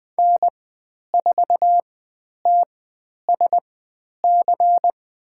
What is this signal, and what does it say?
Morse code, heard as a keyed tone, about 20 wpm, spaced as for 10 wpm, 700 Hz. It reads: N4TSC
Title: 109 Lesson 18g - Letter m - Call Signs 20wpm